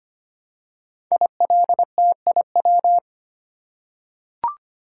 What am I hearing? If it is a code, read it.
Morse code, heard as a keyed tone, about 25 wpm, 700 Hz. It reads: ILTIW
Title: Code Groups - 5 Letters - 25wpm